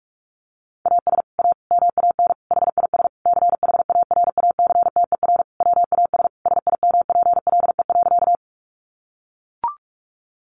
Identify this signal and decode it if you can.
Morse code, heard as a keyed tone, about 44 wpm, 700 Hz. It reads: AS A MAN HIS CHARACTER WAS SIMPLE.
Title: Sentences from Top 1200 Words 45wpm